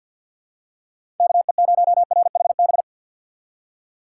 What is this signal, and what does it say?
Morse code, heard as a keyed tone, about 46 wpm, 700 Hz. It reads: KE0RHB
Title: USA Call Signs - 6 Characters - 3x Character Spacing - 50wpm